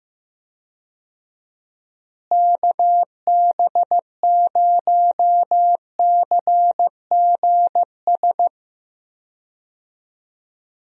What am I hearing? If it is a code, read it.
Morse code, heard as a keyed tone, about 15 wpm, 700 Hz. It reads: KB0CGS
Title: USA Call Signs - 6 Characters - Repeated 2x - 15wpm